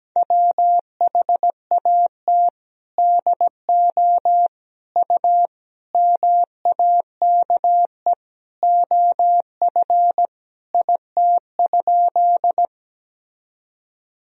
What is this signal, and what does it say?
Morse code, heard as a keyed tone, about 17 wpm, 700 Hz. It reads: WHAT DO U MAKE OF IT?
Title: Sentences from Top 100 Words and Abbreviations 17wpm